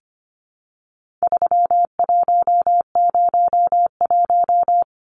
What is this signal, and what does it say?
Morse code, heard as a keyed tone, about 25 wpm, 700 Hz. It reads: 3101
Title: Numbers - 4-Digit 25wpm